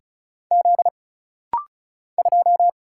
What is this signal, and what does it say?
Morse code, heard as a keyed tone, about 34 wpm, 700 Hz. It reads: Z 2
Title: Single Letter-Number Alt - Rapid-Fire 35wpm